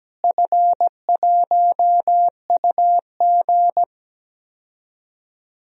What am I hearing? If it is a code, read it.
Morse code, heard as a keyed tone, about 17 wpm, 700 Hz. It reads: F1UG
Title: DXCC-100 Call Signs - 4 Characters - Repeated 3x - 17wpm